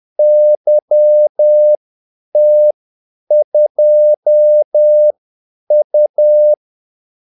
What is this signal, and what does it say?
Morse code, heard as a keyed tone, about 10 wpm, spaced as for 8 wpm, 600 Hz. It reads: YT2U
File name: Callsigns-10-8wpm